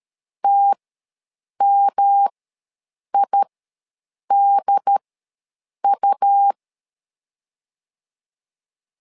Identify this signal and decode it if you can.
Morse code, heard as a keyed tone, about 13 wpm, spaced as for 7 wpm, 800 Hz. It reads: TMIDU
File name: LEZIONI / 3 MWRO / 3 M ws 12wpm 1745